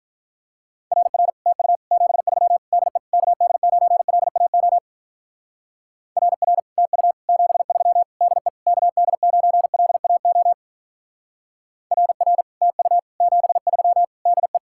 Morse code, heard as a keyed tone, about 46 wpm, 700 Hz. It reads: RR TU 73 DE KD9LAO RR TU 73 DE KD9LAO RR TU 73 DE